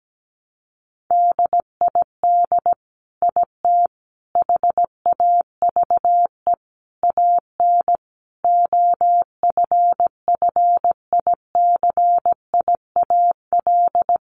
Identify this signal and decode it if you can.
Morse code, heard as a keyed tone, about 17 wpm, 700 Hz. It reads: DID IT HAVE AN OFFICIAL
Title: Sentences from Top 1200 Words 17wpm